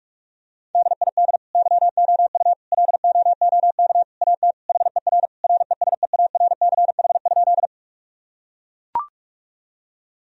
Morse code, heard as a keyed tone, about 41 wpm, 700 Hz. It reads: DID YOU LOOK AT HER RESEARCH?
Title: Sentences from Top 1000 Words - Review Part 4 - 45wpm